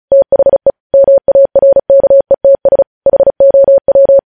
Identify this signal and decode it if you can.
Morse code, heard as a keyed tone, about 36 wpm, 550 Hz. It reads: THE MARKETS HOW
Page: news headlines in morse code at 35 wpm